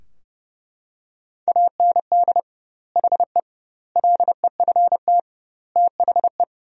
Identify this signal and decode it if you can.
Morse code, heard as a keyed tone, about 30 wpm, 700 Hz. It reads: AND HE LEFT THE
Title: 2x Word-Spacing - Sentences from Top 300 Words - 30wpm